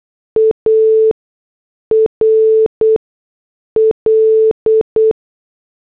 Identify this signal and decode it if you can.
Morse code, heard as a keyed tone, about 8 wpm, spaced as for 6 wpm, 440 Hz. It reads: ARL